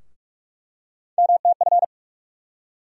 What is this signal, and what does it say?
Morse code, heard as a keyed tone, about 42 wpm, 700 Hz. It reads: MTF
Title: USA Call Sign Postfixes - 3 Letters 45wpm